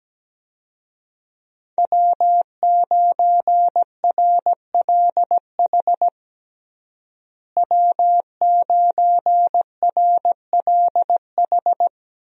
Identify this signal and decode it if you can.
Morse code, heard as a keyed tone, about 17 wpm, 700 Hz. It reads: W9RLH W9RLH
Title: USA Call Signs - 5 Characters - Repeated 2x - 17wpm